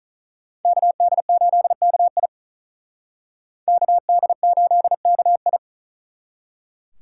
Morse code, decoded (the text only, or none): KD8KI KD8KI